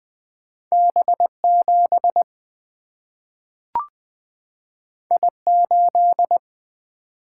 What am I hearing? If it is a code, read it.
Morse code, heard as a keyed tone, about 20 wpm, 700 Hz. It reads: B7 I8
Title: Single-Letter Single-Number 20wpm